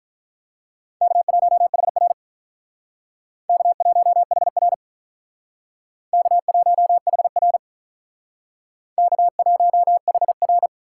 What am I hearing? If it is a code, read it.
Morse code, heard as a keyed tone, about 44 wpm, 700 Hz. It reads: K1HR K1HR K1HR K1HR